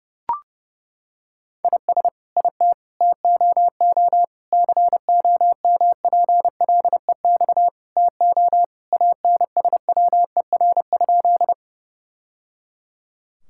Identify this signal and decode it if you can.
Morse code, heard as a keyed tone, about 32 wpm, 700 Hz. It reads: IS IT TOO COMPLEX TO ANSWER?
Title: Sentences from Top 1000 Words - Review Part 6 - 30wpm